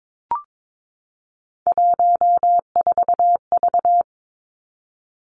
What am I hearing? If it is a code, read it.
Morse code, heard as a keyed tone, about 22 wpm, 700 Hz. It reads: 14V
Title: Three Challenging Letter-Number Combinations 22wpm